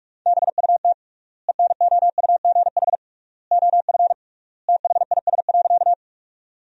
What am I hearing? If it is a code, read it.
Morse code, heard as a keyed tone, about 41 wpm, 700 Hz. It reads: BUT ENOUGH OF THIS.
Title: 3x Word-Spacing - Sentences from Top 400 Words - 45wpm